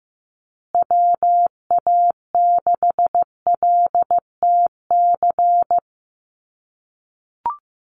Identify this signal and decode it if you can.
Morse code, heard as a keyed tone, about 15 wpm, 700 Hz. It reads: WA6LTC
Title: USA Call Signs Repeated 3x 15wpm